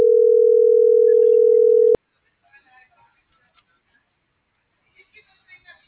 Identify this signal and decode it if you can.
Ringback tone (North America), 440 + 480 Hz.